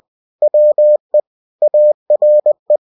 Morse code, heard as a keyed tone, about 20 wpm, 600 Hz. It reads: WE ARE